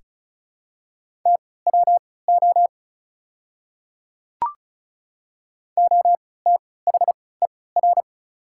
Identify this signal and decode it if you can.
Morse code, heard as a keyed tone, about 35 wpm, spaced as for 20 wpm, 700 Hz. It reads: TWO OTHER
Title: 3x Character Spacing - Top 100 Words - 35wpm